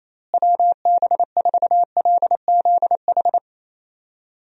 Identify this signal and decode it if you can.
Morse code, heard as a keyed tone, about 27 wpm, 700 Hz. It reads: WB4LZH
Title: USA Call Signs - 6 Characters - Repeated 3x - 28wpm